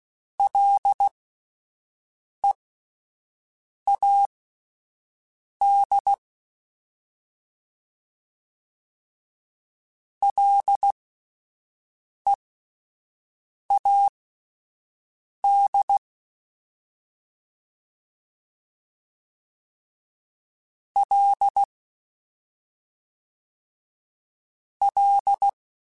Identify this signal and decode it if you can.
Morse code, heard as a keyed tone, about 16 wpm, spaced as for 5 wpm, 800 Hz. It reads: LEAD LEAD L L
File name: Ls / Single L m3s / Single L Prefixe 3x2 Twice @5